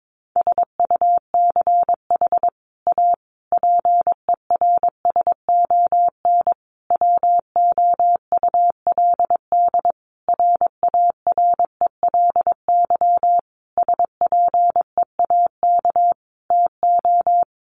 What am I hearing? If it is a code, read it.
Morse code, heard as a keyed tone, about 22 wpm, 700 Hz. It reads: SUCH A PERSON WOULD RARELY SPEAK TO
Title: Sentences from Top 1400 Words 22wpm